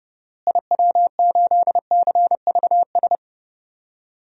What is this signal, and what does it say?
Morse code, heard as a keyed tone, about 32 wpm, 700 Hz. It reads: IW8CVS
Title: DXCC-100 Call Signs - 6 Characters - Repeated 2x - 30wpm